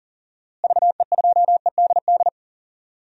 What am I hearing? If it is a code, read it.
Morse code, heard as a keyed tone, about 37 wpm, 700 Hz. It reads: VE2EDD
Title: DXCC-100 Call Signs - 6 Characters - Repeated 2x - 40wpm